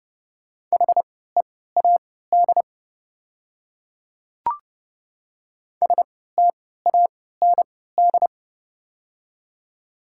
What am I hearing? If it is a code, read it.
Morse code, heard as a keyed tone, about 30 wpm, spaced as for 17 wpm, 700 Hz. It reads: HEAD STAND
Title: 3x Character Spacing - Top 500 Words - 30wpm